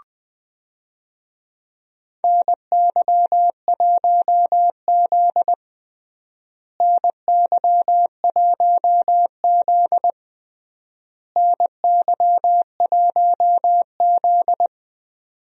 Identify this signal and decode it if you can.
Morse code, heard as a keyed tone, about 20 wpm, 700 Hz. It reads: NY1Z NY1Z NY1Z
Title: USA Call Signs - 4 Characters - Repeated 3x - 20wpm